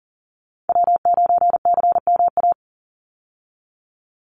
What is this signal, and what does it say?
Morse code, heard as a keyed tone, about 40 wpm, 700 Hz. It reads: W9CMA
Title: USA Call Signs 40wpm